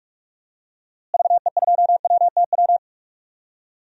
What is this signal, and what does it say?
Morse code, heard as a keyed tone, about 41 wpm, 700 Hz. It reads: VE2WTW
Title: DXCC-100 Call Signs - 6 Characters - Repeated 2x - 45wpm